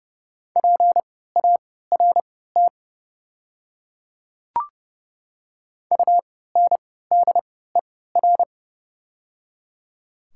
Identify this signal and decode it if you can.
Morse code, heard as a keyed tone, about 30 wpm, spaced as for 17 wpm, 700 Hz. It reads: PART UNDER
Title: 3x Character Spacing - Top 300 Words - 30wpm